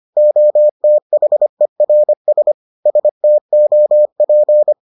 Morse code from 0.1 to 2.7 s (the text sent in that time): OTHERS